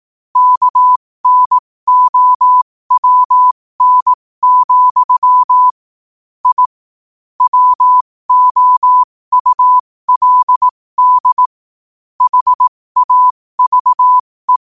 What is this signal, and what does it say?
Morse code, heard as a keyed tone, about 18 wpm, spaced as for 15 wpm, 1000 Hz. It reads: KNOWN, I WOULD HAVE
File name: qotdfiles / 15wpm